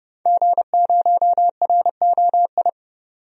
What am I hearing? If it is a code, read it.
Morse code, heard as a keyed tone, about 30 wpm, 700 Hz. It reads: G0ROI
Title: DXCC-100 Call Signs - 5 Characters - 30wpm